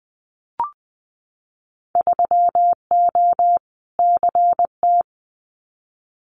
Morse code, heard as a keyed tone, about 20 wpm, 700 Hz. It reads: 3O CT
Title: Winter Field Day Exchanges 20wpm